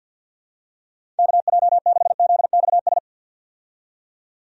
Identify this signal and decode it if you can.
Morse code, heard as a keyed tone, about 51 wpm, 700 Hz. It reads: KJ6ZXS
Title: USA Call Signs - 6 Characters - 50wpm